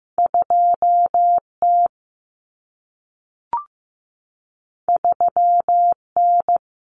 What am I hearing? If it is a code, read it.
Morse code, heard as a keyed tone, about 15 wpm, 700 Hz. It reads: 2T 3N
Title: Common Cut Numbers - 2-Digit 15wpm